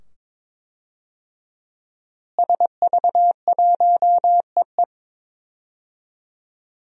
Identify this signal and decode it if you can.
Morse code, heard as a keyed tone, about 22 wpm, 700 Hz. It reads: SV1EE
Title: DXCC-100 Call Signs - 5 Characters - Repeated 2x - 22wpm